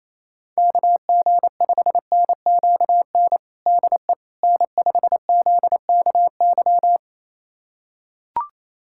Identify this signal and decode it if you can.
Morse code, heard as a keyed tone, about 27 wpm, 700 Hz. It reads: KG5NQN DE N5ZKY
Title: USA Callsign DE Callsign 28wpm